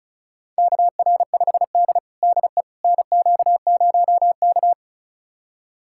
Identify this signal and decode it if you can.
Morse code, heard as a keyed tone, about 36 wpm, 700 Hz. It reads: KR5D DE NQ0K
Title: USA Callsign DE Callsign - 4 Characters - 35wpm